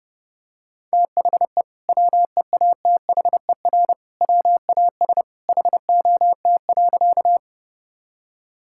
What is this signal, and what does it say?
Morse code, heard as a keyed tone, about 32 wpm, 700 Hz. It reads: THE WEATHER WAS HOT.
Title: Sentences from Top 1000 Words - Review Part 5 - 30wpm